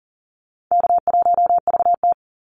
Morse code, heard as a keyed tone, about 40 wpm, 700 Hz. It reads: K1VT
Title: Easy Call Signs Repeated 3x - 40wpm